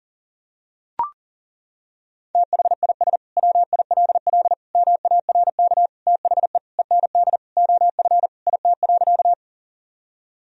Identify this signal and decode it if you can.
Morse code, heard as a keyed tone, about 40 wpm, 700 Hz. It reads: THIS WILL MARK THE END OF IT.
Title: Sentences from Top 1000 Words - Review Part 1 - 40wpm